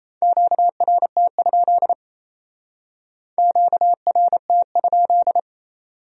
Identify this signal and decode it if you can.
Morse code, heard as a keyed tone, about 36 wpm, 700 Hz. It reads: QRT? QRT?